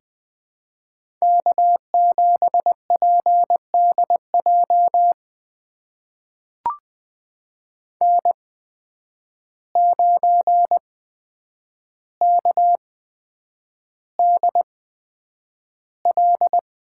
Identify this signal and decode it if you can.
Morse code, heard as a keyed tone, about 21 wpm, spaced as for 9 wpm, 700 Hz. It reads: K7PDJ N9KDL
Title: USA Call Signs - 5 Characters - 8x Character Spacing - 20wpm